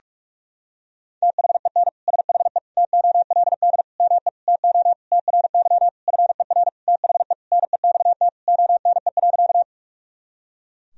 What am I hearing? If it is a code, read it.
Morse code, heard as a keyed tone, about 46 wpm, 700 Hz. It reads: THEN SHE TOLD ME TO TRY FER THE NEXT ONE.
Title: Sentences from Top 400 Words and Abbreviations 45wpm